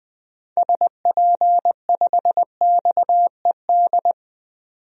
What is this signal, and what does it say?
Morse code, heard as a keyed tone, about 20 wpm, 700 Hz. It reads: SP5XED
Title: DXCC-100 Call Signs - 6 Characters - 20wpm